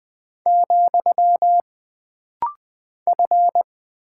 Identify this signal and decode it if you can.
Morse code, heard as a keyed tone, about 20 wpm, 700 Hz. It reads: , F
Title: Single Character (Letter:Number:Punctuation:Prosign) - Rapid-Fire - 20wpm